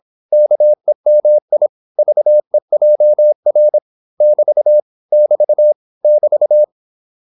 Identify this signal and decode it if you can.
Morse code, heard as a keyed tone, about 26 wpm, 600 Hz. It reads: KEMI VEJR <BT> <BT> <BT>
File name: Korte ord DK 26wpm